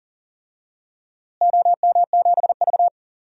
Morse code, heard as a keyed tone, about 40 wpm, 700 Hz. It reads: OM7V
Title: DXCC-100 Call Signs - 4 Characters - Repeated 3x - 40wpm